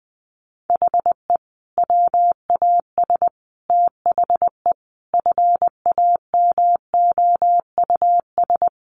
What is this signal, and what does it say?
Morse code, heard as a keyed tone, about 20 wpm, 700 Hz. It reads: HE WAS THE FAMOUS
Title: Sentences from Top 1500 Words 20wpm